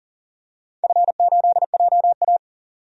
Morse code, heard as a keyed tone, about 41 wpm, 700 Hz. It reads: F8JA
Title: DXCC-100 Call Signs - 4 Characters - 40wpm